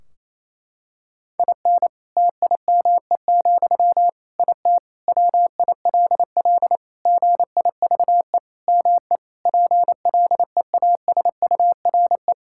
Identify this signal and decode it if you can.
Morse code, heard as a keyed tone, about 28 wpm, 700 Hz. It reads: IN TIME, IT WILL GIVE ME PLEASURE
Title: Sentences from Top 1600 Words 28wpm